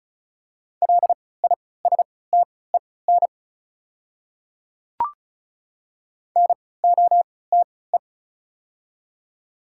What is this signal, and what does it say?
Morse code, heard as a keyed tone, about 34 wpm, spaced as for 20 wpm, 690 Hz. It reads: LISTEN NOTE
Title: 3x Character Spacing - Top 500 Words - 35wpm